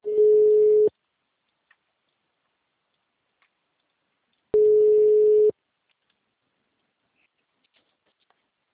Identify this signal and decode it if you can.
Ringback tone (Europe, 425 Hz).
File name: out- -2002- - - .12347